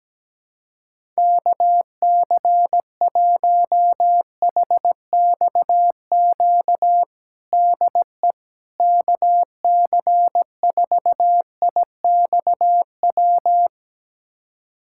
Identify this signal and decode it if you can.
Morse code, heard as a keyed tone, about 17 wpm, 700 Hz. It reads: KC1HXQ DE KC4IXW
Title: USA Callsign DE Callsign - 6 Characters - 17wpm